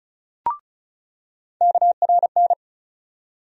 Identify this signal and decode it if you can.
Morse code, heard as a keyed tone, about 36 wpm, 700 Hz. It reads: KRN
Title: USA Call Sign Postfixes - 3 Letters 35wpm